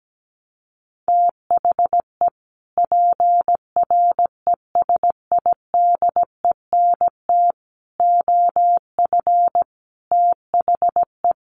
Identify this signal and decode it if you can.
Morse code, heard as a keyed tone, about 17 wpm, 700 Hz. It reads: THE PRESIDENT OF THE
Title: Sets of 5 Words 17wpm